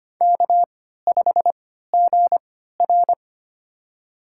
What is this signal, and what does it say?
Morse code, heard as a keyed tone, about 25 wpm, spaced as for 14 wpm, 700 Hz. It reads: K5GR
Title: USA Call Signs - 3x Character Spacing - 25wpm